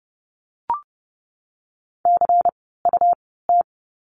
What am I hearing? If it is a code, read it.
Morse code, heard as a keyed tone, about 30 wpm, spaced as for 17 wpm, 700 Hz. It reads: CUT
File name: Lesson 17f - Letter u - Words 30wpm